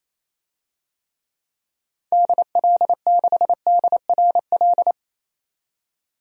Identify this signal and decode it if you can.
Morse code, heard as a keyed tone, about 28 wpm, 700 Hz. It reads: DL6DRL
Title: DXCC-100 Call Signs - 6 Characters - 28wpm